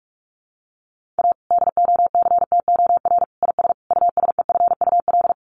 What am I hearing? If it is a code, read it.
Morse code, heard as a keyed tone, about 44 wpm, 700 Hz. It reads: A DOCTOR IS USEFUL
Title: Sentences from Top 1500 Words 45wpm